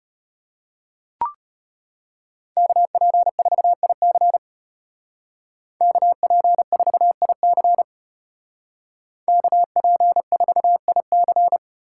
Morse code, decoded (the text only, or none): KP4IC KP4IC KP4IC